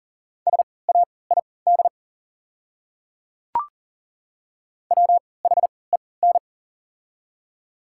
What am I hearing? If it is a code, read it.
Morse code, heard as a keyed tone, about 40 wpm, spaced as for 23 wpm, 690 Hz. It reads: SAID WHEN
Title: 3x Character Spacing - Top 100 Words - 40wpm